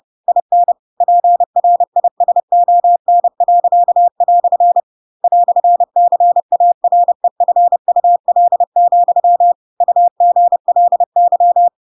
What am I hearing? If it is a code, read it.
Morse code, heard as a keyed tone, about 30 wpm, 700 Hz. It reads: IN PRISON." "CAREFUL, UGLY
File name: Pinocchio / Pinocchio ch4 30 30 ik1hge